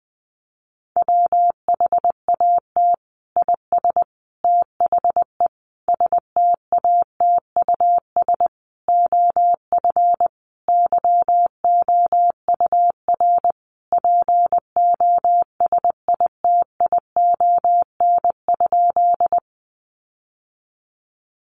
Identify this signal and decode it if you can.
Morse code, heard as a keyed tone, about 20 wpm, 700 Hz. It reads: WHAT IS THE STATUS OF YOUR POSITION?
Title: Sentences from Top 1400 Words 20wpm